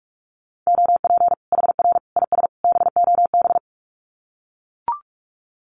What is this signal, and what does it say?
Morse code, heard as a keyed tone, about 44 wpm, 700 Hz. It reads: OP HR IS BOB
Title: Common QSO Phrases 45wpm